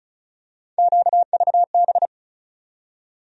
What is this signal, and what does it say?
Morse code, heard as a keyed tone, about 36 wpm, 700 Hz. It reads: QVB
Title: USA Call Sign Postfixes - 3 Letters 35wpm